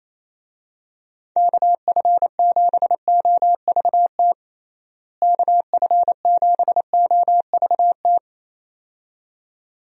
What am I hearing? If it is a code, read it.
Morse code, heard as a keyed tone, about 28 wpm, 700 Hz. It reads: KF7OVT KF7OVT
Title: USA Call Signs - 6 Characters - Repeated 2x - 28wpm